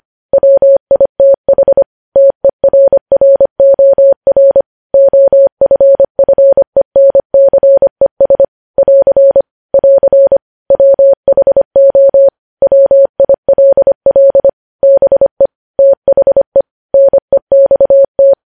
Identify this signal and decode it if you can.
Morse code, heard as a keyed tone, about 25 wpm, 550 Hz. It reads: WITH TERROR OFFENCES <AR> <AR> WHO WILL BE THE NEXT
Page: news headlines in morse code at 25 wpm